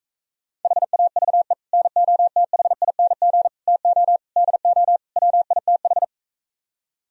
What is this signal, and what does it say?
Morse code, heard as a keyed tone, about 45 wpm, 700 Hz. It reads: HAVE NOTHING TO DO WITH